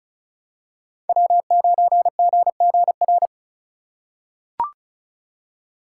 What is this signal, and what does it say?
Morse code, heard as a keyed tone, about 36 wpm, 700 Hz. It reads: W9GGR
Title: USA Call Signs - 5 Characters - 2x Character Spacing - 35wpm